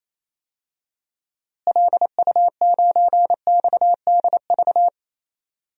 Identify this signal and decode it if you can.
Morse code, heard as a keyed tone, about 28 wpm, 700 Hz. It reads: LU9XDV
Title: DXCC-100 Call Signs - 6 Characters - Repeated 3x - 28wpm